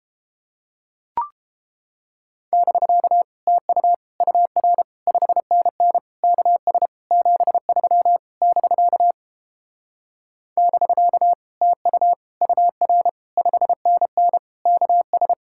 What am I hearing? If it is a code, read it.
Morse code, heard as a keyed tone, about 29 wpm, 700 Hz. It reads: <BK> TU UR 5NN KS 73 <BK> <BK> TU UR 5NN KS